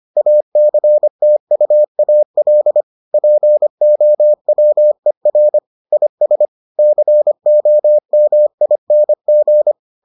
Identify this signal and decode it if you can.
Morse code, heard as a keyed tone, about 25 wpm, 600 Hz. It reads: ACTUAL POWER IS COMING